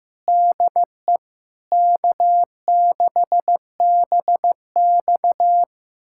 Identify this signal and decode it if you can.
Morse code, heard as a keyed tone, about 15 wpm, 700 Hz. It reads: DE K6BX
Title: USA Callsign DE Callsign - 4 Characters - 15wpm